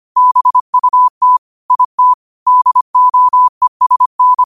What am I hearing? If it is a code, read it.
Morse code, heard as a keyed tone, about 25 wpm, 1000 Hz. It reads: BUT IT DOESN
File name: qotdfiles / 25wpm